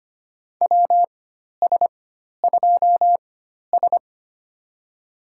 Morse code, heard as a keyed tone, about 25 wpm, spaced as for 12 wpm, 700 Hz. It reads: WS2S
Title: USA Call Signs - 4 Characters - 4x Character Spacing - 25wpm